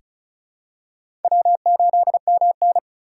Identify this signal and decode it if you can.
Morse code, heard as a keyed tone, about 36 wpm, 700 Hz. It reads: W8MN
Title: USA Call Signs - 4 Characters - 3x Character Spacing - 35wpm